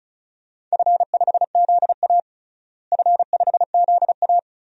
Morse code, heard as a keyed tone, about 34 wpm, 700 Hz. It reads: F5ZA F5ZA
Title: DXCC-100 Call Signs - 4 Characters - Repeated 3x - 35wpm